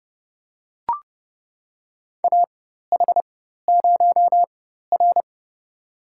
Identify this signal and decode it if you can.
Morse code, heard as a keyed tone, about 30 wpm, spaced as for 14 wpm, 700 Hz. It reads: AH0R
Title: USA Call Signs - 4 Characters - 4x Character Spacing - 30wpm